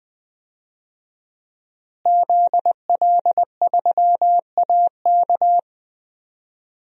Morse code, heard as a keyed tone, about 20 wpm, 700 Hz. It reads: ZL3AK
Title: DXCC-100 Call Signs - 5 Characters - Repeated 2x - 20wpm